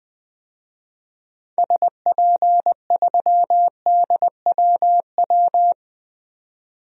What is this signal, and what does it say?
Morse code, heard as a keyed tone, about 20 wpm, 700 Hz. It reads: SP3DWW
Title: DXCC-100 Call Signs - 6 Characters - 20wpm